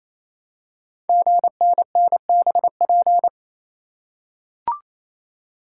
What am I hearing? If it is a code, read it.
Morse code, heard as a keyed tone, about 27 wpm, 700 Hz. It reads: GNNBP
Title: Code Groups - 5 Letters - 28wpm